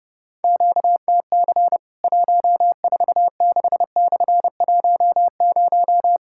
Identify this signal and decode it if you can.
Morse code, heard as a keyed tone, about 30 wpm, 700 Hz. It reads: QTC 146/10
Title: Worked All Europe DX Contest - QTC Exchange - 30wpm